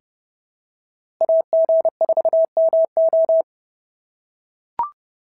Morse code, heard as a keyed tone, about 30 wpm, 690 Hz. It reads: AG4MO
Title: SOTA QSO Practice 30wpm